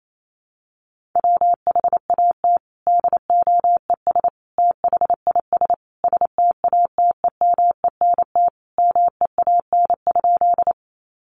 Morse code, heard as a keyed tone, about 28 wpm, 700 Hz. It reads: WHAT DOES THIS STATEMENT MEAN?
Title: Sentences from Top 1200 Words 28wpm